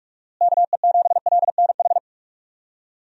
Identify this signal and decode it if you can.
Morse code, heard as a keyed tone, about 44 wpm, 700 Hz. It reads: KE7LNH
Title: USA Call Signs - 6 Characters - Repeated 3x - 45wpm